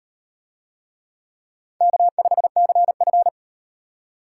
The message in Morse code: K5CF

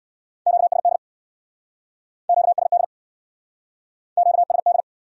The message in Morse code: QSD QSD QSD

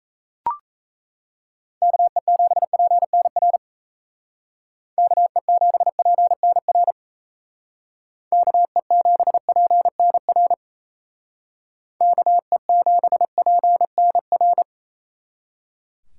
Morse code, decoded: KE7PNR KE7PNR KE7PNR KE7PNR

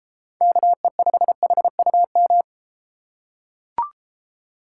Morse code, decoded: KE5HUM

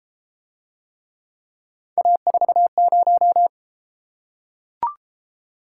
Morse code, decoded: A40

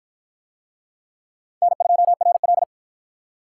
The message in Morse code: N2RL

